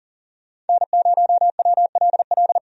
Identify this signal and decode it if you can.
Morse code, heard as a keyed tone, about 40 wpm, 700 Hz. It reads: N0WLL
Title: USA Call Signs - 5 Characters - 40wpm